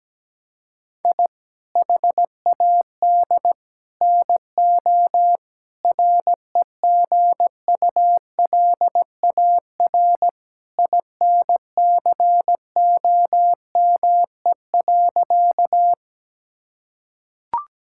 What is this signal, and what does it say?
Morse code, heard as a keyed tone, about 17 wpm, 700 Hz. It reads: I HAD NO REGULAR INCOME.
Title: Sentences from Top 1000 Words - Review Part 6 - 17wpm